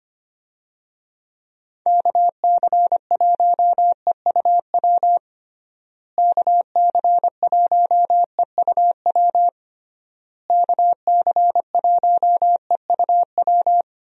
Morse code, heard as a keyed tone, about 25 wpm, 700 Hz. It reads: KC1EUW KC1EUW KC1EUW
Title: USA Call Signs - 6 Characters - Repeated 3x - 25wpm